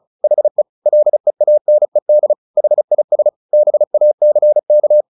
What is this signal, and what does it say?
Morse code, heard as a keyed tone, about 34 wpm, 600 Hz. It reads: HE LEANED HIS BACK
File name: the old man and the sea 2 35wpm